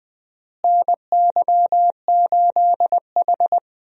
Morse code, heard as a keyed tone, about 20 wpm, 700 Hz. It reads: NY8H
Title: USA Call Signs - 4 Characters - Repeated 3x - 20wpm